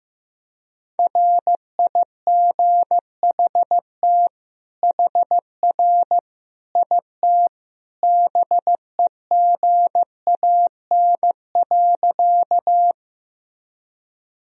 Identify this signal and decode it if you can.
Morse code, heard as a keyed tone, about 15 wpm, 700 Hz. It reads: RIGHT HR IT BEGAN.
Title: Sentences from Top 300 Words and Abbreviations 15wpm